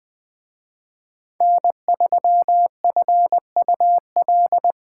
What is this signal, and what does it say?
Morse code, heard as a keyed tone, about 24 wpm, 700 Hz. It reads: N3FUL
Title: Speed Racing - USA 1x3 Call Signs 20wpm